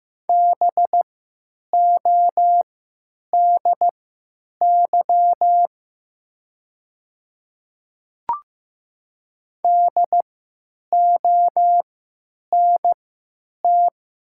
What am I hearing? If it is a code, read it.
Morse code, heard as a keyed tone, about 15 wpm, spaced as for 9 wpm, 700 Hz. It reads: BODY DONT
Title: 3x Character Spacing - Top 500 Words - 15wpm